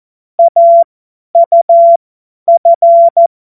Morse code, heard as a keyed tone, about 14 wpm, spaced as for 10 wpm, 680 Hz. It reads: AUF